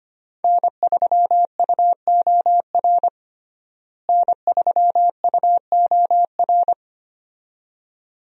Morse code, heard as a keyed tone, about 25 wpm, 700 Hz. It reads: N3UOR N3UOR
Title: USA Call Signs - 5 Characters - Repeated 2x - 25wpm